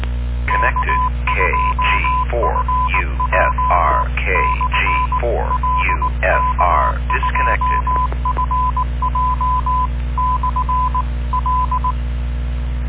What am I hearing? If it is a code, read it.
Morse code, heard as a keyed tone, about 19 wpm, 1010 Hz, with noise. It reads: K2FJ/L K2FJ/L